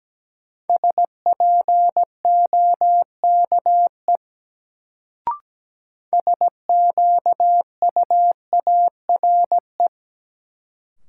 Morse code, heard as a keyed tone, about 17 wpm, 700 Hz. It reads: SPOKE SQUARE
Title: Sentences from Top 1200 Words - New Word Drill Alt - 17wpm